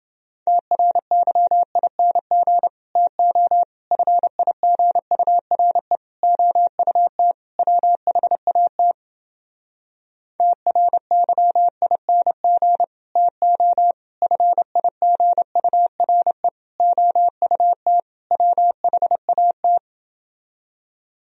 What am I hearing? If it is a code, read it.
Morse code, heard as a keyed tone, about 30 wpm, 700 Hz. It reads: TRYING TO FIGURE OUT WHAT TRYING TO FIGURE OUT WHAT